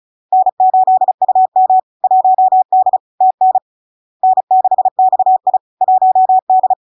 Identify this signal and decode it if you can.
Morse code, heard as a keyed tone, about 36 wpm, 750 Hz. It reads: N8UM 1D TN N6XI 1D
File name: LARK-FD-2022-Log-in-CW-35wpm-750-Hz